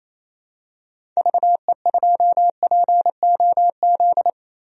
Morse code, heard as a keyed tone, about 28 wpm, 700 Hz. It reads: VE2POZ
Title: DXCC-100 Call Signs - 6 Characters - Repeated 2x - 28wpm